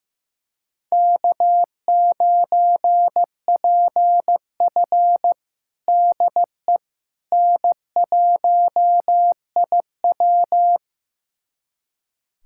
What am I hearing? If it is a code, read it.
Morse code, heard as a keyed tone, about 15 wpm, 700 Hz. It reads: K9PF DE N1IW
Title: USA Callsign DE Callsign - 4 Characters - 15wpm